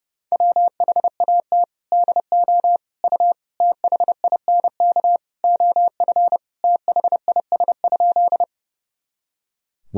Morse code, heard as a keyed tone, about 30 wpm, 700 Hz. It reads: WHAT DO U THINK OF THIS?
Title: Sentences from Top 200 Words and Abbreviations 30wpm